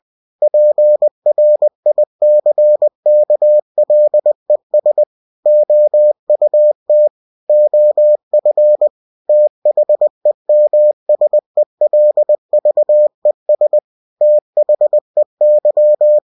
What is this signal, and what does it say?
Morse code, heard as a keyed tone, about 20 wpm, 600 Hz. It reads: PRICKLES OUT OF THEMSELVES THEY
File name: cw-books / winnie-the-pooh / winnie-20wpm-page-0036